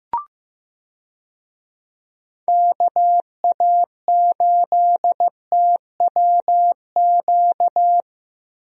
Morse code, heard as a keyed tone, about 15 wpm, 700 Hz. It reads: KA8TWQ
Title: USA Call Signs - 6 Characters - Repeated 3x - 15wpm